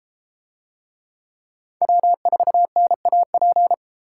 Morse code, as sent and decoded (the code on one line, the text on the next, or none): .-- ....- -. .- .--.
W4NAP